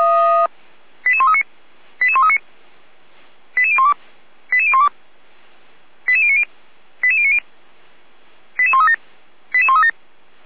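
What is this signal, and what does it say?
Five-tone selective call (ZVEI), heard as quick runs of five tones, tones of about 70 ms. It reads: 88139 88139 88012 88012 88090 88090 88137 88137
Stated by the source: ZVEI five-tone selective call received by SDR, from the Feuerwehr Orlamünde fire brigade audio server.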